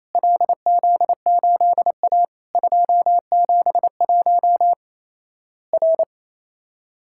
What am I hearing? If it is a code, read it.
Morse code, heard as a keyed tone, about 28 wpm, 700 Hz. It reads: LZ8A 271 R
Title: Worked All Europe DX Contest - QTC Exchange - 28wpm